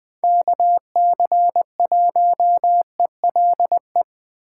Morse code, heard as a keyed tone, about 21 wpm, 700 Hz. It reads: KC1ELE
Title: USA Call Signs - 6 Characters - 3x Character Spacing - 20wpm